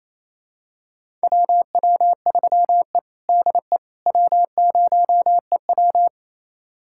Morse code, heard as a keyed tone, about 27 wpm, 700 Hz. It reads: WW3E DE W0EW
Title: USA Callsign DE Callsign - 4 Characters - 28wpm